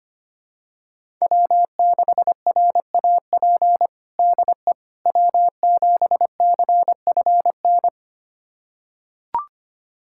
Morse code, heard as a keyed tone, about 25 wpm, 700 Hz. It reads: W6RAP DE W7CFN
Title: USA Callsign DE Callsign - 5 Characters - 25wpm